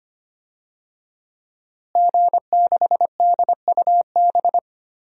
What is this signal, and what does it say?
Morse code, heard as a keyed tone, about 25 wpm, 700 Hz. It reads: G6DUB
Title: DXCC-100 Call Signs - 5 Characters - Repeated 2x - 25wpm